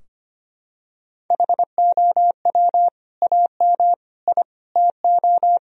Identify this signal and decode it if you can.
Morse code, heard as a keyed tone, about 25 wpm, 700 Hz. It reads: HOW AM I TO
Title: Sentences from Top 1000 Words - Review Part 4 - 25wpm